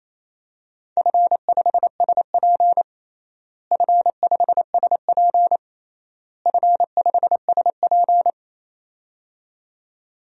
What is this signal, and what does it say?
Morse code, heard as a keyed tone, about 28 wpm, 700 Hz. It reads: F5SP F5SP F5SP
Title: DXCC-100 Call Signs - 4 Characters - Repeated 3x - 28wpm